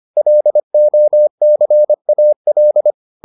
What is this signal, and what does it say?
Morse code, heard as a keyed tone, about 25 wpm, 600 Hz. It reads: LOCAL